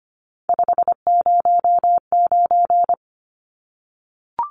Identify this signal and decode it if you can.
Morse code, heard as a keyed tone, about 25 wpm, 700 Hz. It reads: 509
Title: Numbers - 3-Digit 25wpm